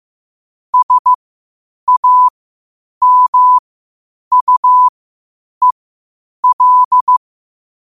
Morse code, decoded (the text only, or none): SAMUEL